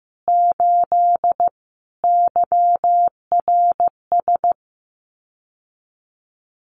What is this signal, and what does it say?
Morse code, heard as a keyed tone, about 15 wpm, 700 Hz. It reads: 8 YRS
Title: Common QSO Phrases Encore 15wpm